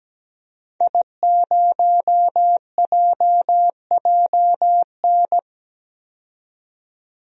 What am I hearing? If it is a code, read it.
Morse code, heard as a keyed tone, about 17 wpm, 700 Hz. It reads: I0JJN
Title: DXCC-100 Call Signs - 5 Characters - 17wpm